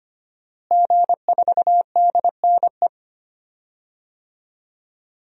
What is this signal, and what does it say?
Morse code, heard as a keyed tone, about 25 wpm, 700 Hz. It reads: G4DNE
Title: DXCC-100 Call Signs - 5 Characters - Repeated 3x - 25wpm